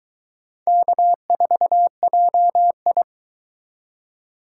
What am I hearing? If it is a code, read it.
Morse code, heard as a keyed tone, about 22 wpm, 700 Hz. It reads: K4JI